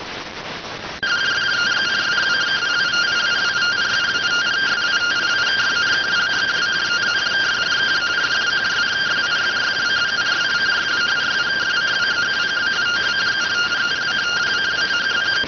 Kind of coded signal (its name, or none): two-tone teleprinter signal